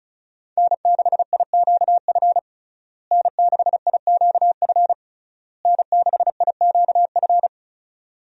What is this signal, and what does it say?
Morse code, heard as a keyed tone, about 36 wpm, 700 Hz. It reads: N6IQF N6IQF N6IQF
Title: USA Call Signs - 5 Characters - Repeated 3x - 35wpm